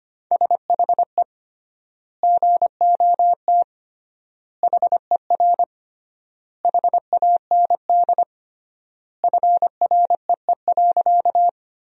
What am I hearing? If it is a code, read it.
Morse code, heard as a keyed tone, about 25 wpm, 700 Hz. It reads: SHE GOT HER HAND FREE.
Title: 3x Word-Spacing - Sentences from Top 500 Words - 25wpm